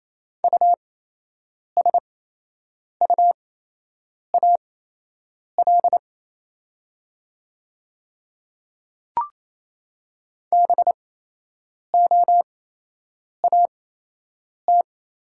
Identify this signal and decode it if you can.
Morse code, heard as a keyed tone, about 28 wpm, spaced as for 8 wpm, 700 Hz. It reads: USUAL BOAT
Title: 8x Character Spacing - Top 500 Words - 28wpm